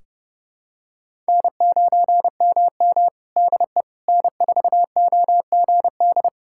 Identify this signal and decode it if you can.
Morse code, heard as a keyed tone, about 30 wpm, 700 Hz. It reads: N9MM DE N4OGD
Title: K1USN SST QSO Practice 30wpm